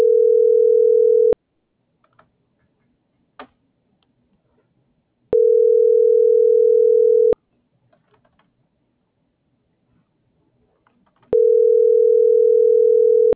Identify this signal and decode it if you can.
Ringback tone (North America), 440 + 480 Hz.